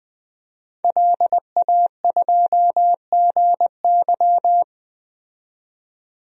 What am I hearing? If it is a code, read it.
Morse code, heard as a keyed tone, about 20 wpm, 700 Hz. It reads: LA2GY
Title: DXCC-100 Call Signs - 5 Characters - 20wpm